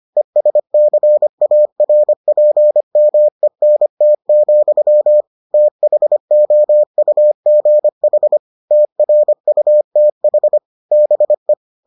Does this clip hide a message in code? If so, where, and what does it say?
Yes: Morse code from 0.2 to 11.5 s, heard as a keyed tone, about 25 wpm, 600 Hz. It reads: ESCARPMENT, THOUGH TRUTH BE